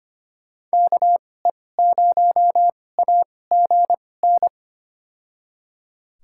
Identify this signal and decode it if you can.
Morse code, heard as a keyed tone, about 25 wpm, spaced as for 18 wpm, 700 Hz. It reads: KE0AGN
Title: USA Call Signs - 6 Characters - 2x Character Spacing - 25wpm